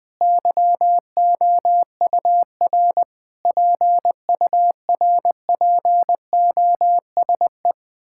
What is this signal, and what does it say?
Morse code, heard as a keyed tone, about 21 wpm, 700 Hz. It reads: YOUR PURPOSE
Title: Sets of 9 Words - 20wpm